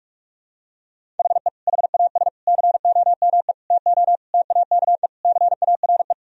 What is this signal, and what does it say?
Morse code, heard as a keyed tone, about 46 wpm, 700 Hz. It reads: HE HAS COME TO TAKE CARE
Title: Sentences from Top 1000 Words - Review Part 1 - 45wpm